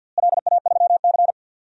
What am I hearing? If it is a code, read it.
Morse code, heard as a keyed tone, about 46 wpm, 700 Hz. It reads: LA3C
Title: DXCC-100 Call Signs - 4 Characters - Repeated 2x - 50wpm